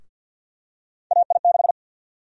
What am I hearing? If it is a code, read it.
Morse code, heard as a keyed tone, about 52 wpm, 700 Hz. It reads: AI6
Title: USA Call Sign Prefixes - 50wpm